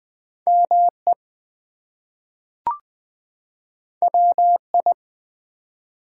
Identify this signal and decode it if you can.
Morse code, heard as a keyed tone, about 20 wpm, 700 Hz. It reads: ME WI
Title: US States and Canadian Provinces 20wpm